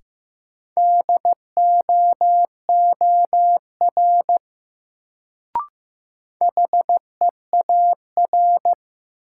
Morse code, heard as a keyed tone, about 15 wpm, 700 Hz. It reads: DOOR HEAR
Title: Sentences from Top 300 Words - New Word Drill - 15wpm